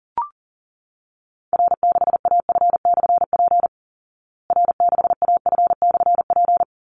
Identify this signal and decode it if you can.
Morse code, heard as a keyed tone, about 40 wpm, 700 Hz. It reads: R6AF/P R6AF/P
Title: Call Signs Repeated 3x - 40wpm